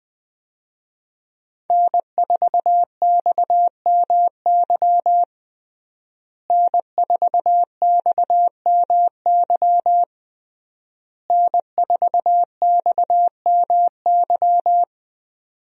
Morse code, heard as a keyed tone, about 20 wpm, 700 Hz. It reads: N4XMY N4XMY N4XMY
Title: USA Call Signs - 5 Characters - Repeated 3x - 20wpm